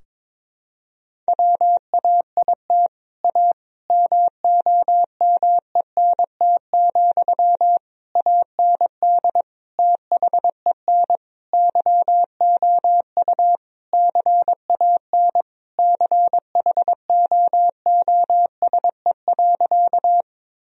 Morse code, heard as a keyed tone, about 22 wpm, 700 Hz. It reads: WAIT A MOMENT, AND THEN YOU CAN CHOOSE.
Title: Sentences from Top 1000 Words - Review Part 4 - 22wpm